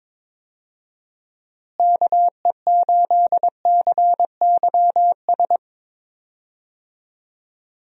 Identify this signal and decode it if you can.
Morse code, heard as a keyed tone, about 22 wpm, 700 Hz. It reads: KE8CYS
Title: USA Call Signs - 6 Characters - Repeated 2x - 22wpm